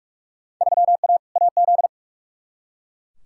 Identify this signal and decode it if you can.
Morse code, heard as a keyed tone, about 43 wpm, 700 Hz. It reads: 3A AZ
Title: Field Day Exchanges - 45wpm